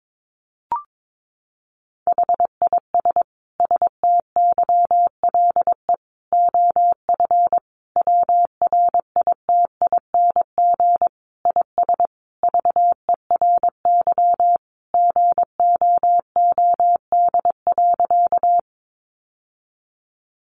Morse code, heard as a keyed tone, about 22 wpm, 700 Hz. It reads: HIS STYLE OF WRITING IS VERY GOOD.
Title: Sentences from Top 800 Words 22wpm